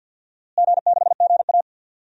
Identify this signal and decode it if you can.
Morse code, heard as a keyed tone, about 52 wpm, 700 Hz. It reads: G6GA
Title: DXCC-100 Call Signs - 4 Characters - Repeated 2x - 50wpm